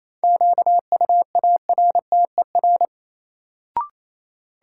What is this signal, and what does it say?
Morse code, heard as a keyed tone, about 28 wpm, 700 Hz. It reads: QUARTER
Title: Sentences from Top 1300 Words - New Word Drill Alt - 28wpm